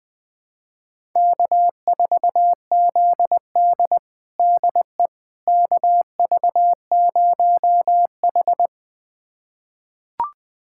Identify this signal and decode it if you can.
Morse code, heard as a keyed tone, about 21 wpm, 700 Hz. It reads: K4ZD DE KV0H
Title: USA Callsign DE Callsign - 4 Characters - 20wpm